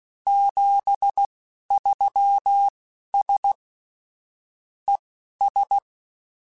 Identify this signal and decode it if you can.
Morse code, heard as a keyed tone, about 16 wpm, spaced as for 11 wpm, 800 Hz. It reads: 73S ES
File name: QSOs / QSOs-mp3s / kc7haa@10